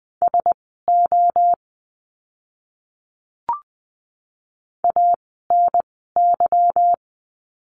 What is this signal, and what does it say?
Morse code, heard as a keyed tone, about 20 wpm, spaced as for 14 wpm, 700 Hz. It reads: SO ANY
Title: 230 Lesson 37f - Letter x - Words 20wpm